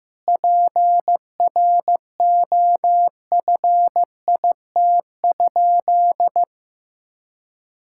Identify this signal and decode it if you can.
Morse code, heard as a keyed tone, about 16 wpm, 700 Hz. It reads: PROFIT?
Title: Sentences from Top 1000 Words - Review Part 5 - 15wpm